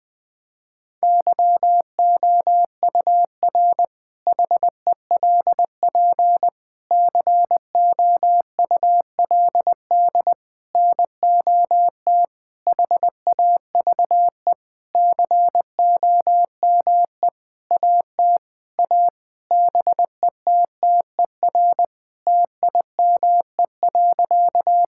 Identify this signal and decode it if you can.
Morse code, heard as a keyed tone, about 20 wpm, 700 Hz. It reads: YOUR HELP COULD NOT HAVE COME AT A BETTER TIME.
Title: Sentences with 8-Word Phrases - 20wpm